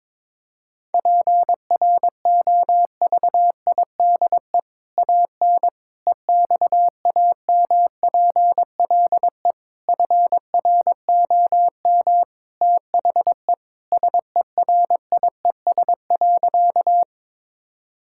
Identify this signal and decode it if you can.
Morse code, heard as a keyed tone, about 22 wpm, 700 Hz. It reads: PROVIDE AN EXAMPLE FROM THE SERIES.
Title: Sentences from Top 1000 Words - Review Part 5 - 22wpm